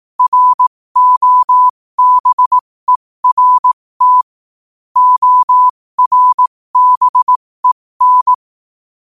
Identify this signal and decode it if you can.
Morse code, heard as a keyed tone, about 18 wpm, spaced as for 15 wpm, 1000 Hz. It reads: ROBERT ORBEN
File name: qotdfiles / 15wpm